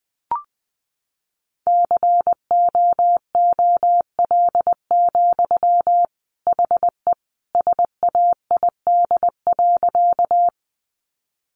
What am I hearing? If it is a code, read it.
Morse code, heard as a keyed tone, about 20 wpm, 700 Hz. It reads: COOL, HE SAID.
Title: Sentences from Top 1200 Words 20wpm